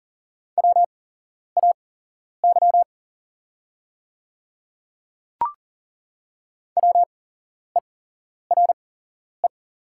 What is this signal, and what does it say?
Morse code, heard as a keyed tone, about 40 wpm, spaced as for 11 wpm, 700 Hz. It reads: WAY WERE
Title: 8x Character Spacing - Top 100 Words - 40wpm